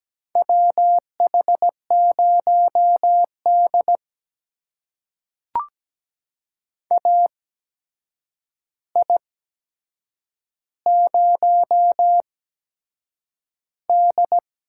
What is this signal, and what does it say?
Morse code, heard as a keyed tone, about 17 wpm, spaced as for 7 wpm, 700 Hz. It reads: WH0D AI0D
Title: USA Call Signs - 4 Characters - 8x Character Spacing - 17wpm